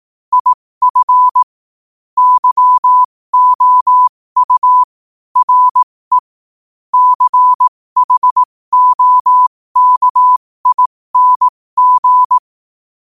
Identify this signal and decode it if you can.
Morse code, heard as a keyed tone, about 18 wpm, spaced as for 15 wpm, 1000 Hz. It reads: IF YOU RE CHOKING
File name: qotdfiles / 15wpm